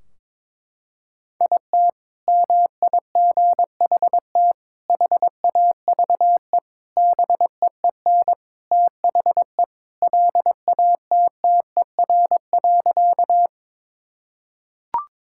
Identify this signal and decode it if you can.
Morse code, heard as a keyed tone, about 22 wpm, 700 Hz. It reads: IT MIGHT HAVE BEEN THE LATTER.
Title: Sentences from Top 1600 Words 22wpm